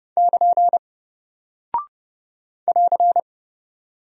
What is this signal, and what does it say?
Morse code, heard as a keyed tone, about 30 wpm, 700 Hz. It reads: <KN> <AR>
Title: Single Character (Letter:Number:Punctuation:Prosign) Alt 30wpm